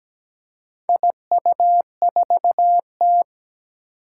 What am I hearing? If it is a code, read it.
Morse code, heard as a keyed tone, about 17 wpm, 700 Hz. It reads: IU4T
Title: Four Challenging Letter-Number Combinations 17wpm